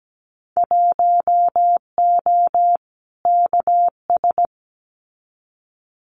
Morse code, heard as a keyed tone, about 17 wpm, 700 Hz. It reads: 1O KS
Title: Winter Field Day Exchanges 17wpm